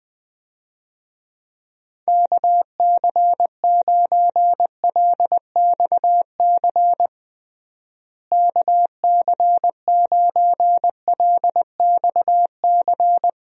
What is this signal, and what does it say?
Morse code, heard as a keyed tone, about 20 wpm, 700 Hz. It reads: KC9LXC KC9LXC
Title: USA Call Signs - 6 Characters - Repeated 2x - 20wpm